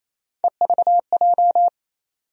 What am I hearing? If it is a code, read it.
Morse code, heard as a keyed tone, about 28 wpm, 700 Hz. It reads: EVJ
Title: USA Call Sign Postfixes - 3 Letters 28wpm